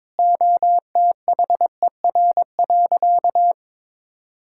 Morse code, heard as a keyed tone, about 22 wpm, 700 Hz. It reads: OTHER.
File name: practice / 22wpm / Sentences with 2-Word Phrases - 22wpm